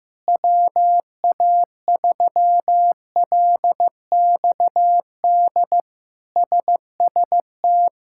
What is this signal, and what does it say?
Morse code, heard as a keyed tone, about 15 wpm, 700 Hz. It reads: WA3LXD SST
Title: K1USN SST QSO Practice - 15wpm